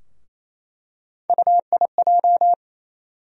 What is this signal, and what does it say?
Morse code, heard as a keyed tone, about 28 wpm, 700 Hz. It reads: UIJ
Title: USA Call Sign Postfixes - 3 Letters 28wpm